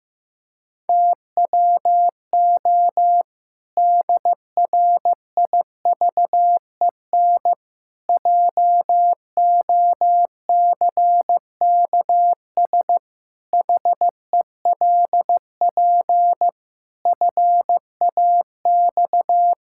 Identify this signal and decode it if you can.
Morse code, heard as a keyed tone, about 15 wpm, 700 Hz. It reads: TWO DRIVEN JOCKS HELP FAX
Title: Pangram Challenge v2 - 15wpm